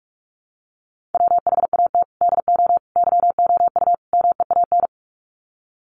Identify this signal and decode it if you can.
Morse code, heard as a keyed tone, about 44 wpm, 700 Hz. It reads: WHAT DO YOU MEAN
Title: Sets of 4 Words 45wpm